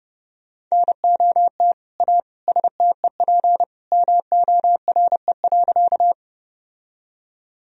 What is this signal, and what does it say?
Morse code, heard as a keyed tone, about 30 wpm, 700 Hz. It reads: NOT A STEP MORE.
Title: Sentences from Top 1000 Words - Review Part 2 - 30wpm